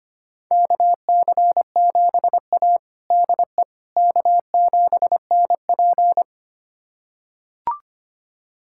Morse code, heard as a keyed tone, about 25 wpm, 700 Hz. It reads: KC7A DE K7NP
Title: USA Callsign DE Callsign - 4 Characters - 25wpm